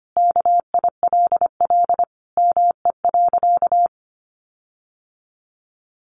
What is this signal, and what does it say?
Morse code, heard as a keyed tone, about 25 wpm, 700 Hz. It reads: KILL ME.
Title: Sentences from Top 1400 Words 25wpm